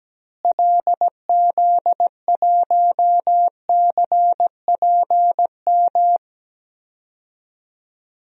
Morse code, heard as a keyed tone, about 17 wpm, 700 Hz. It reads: LZ1CPM
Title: DXCC-100 Call Signs - 6 Characters - Repeated 3x - 17wpm